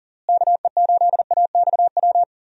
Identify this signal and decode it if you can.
Morse code, heard as a keyed tone, about 38 wpm, 700 Hz. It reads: KE8AXW
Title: USA Call Signs - 6 Characters - Repeated 3x - 40wpm